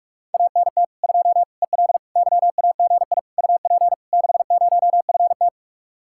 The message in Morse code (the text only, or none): ANT 2 EL YAGI UP 60FT